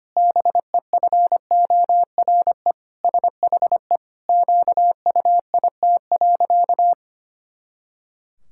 Morse code, heard as a keyed tone, about 25 wpm, 700 Hz. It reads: BEFORE SHE QUIT.
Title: Pangram Challenge v2 - 25wpm